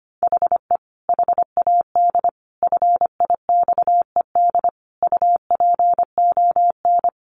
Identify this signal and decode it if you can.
Morse code, heard as a keyed tone, about 25 wpm, 700 Hz. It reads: HE HAD FIXED UPON
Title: Sentences from Top 1500 Words 25wpm